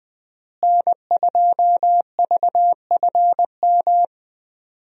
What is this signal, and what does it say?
Morse code, heard as a keyed tone, about 20 wpm, 700 Hz. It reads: N2VFM
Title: USA Call Signs - 5 Characters - Repeated 3x - 20wpm